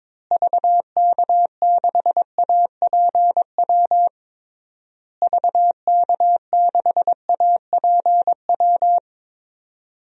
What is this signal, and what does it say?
Morse code, heard as a keyed tone, about 22 wpm, 700 Hz. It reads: VK6APW VK6APW
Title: DXCC-100 Call Signs - 6 Characters - Repeated 3x - 22wpm